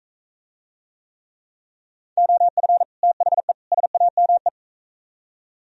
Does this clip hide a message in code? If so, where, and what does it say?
Yes: Morse code from 2.2 to 4.5 s, heard as a keyed tone, about 45 wpm, 700 Hz. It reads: OF THE SAME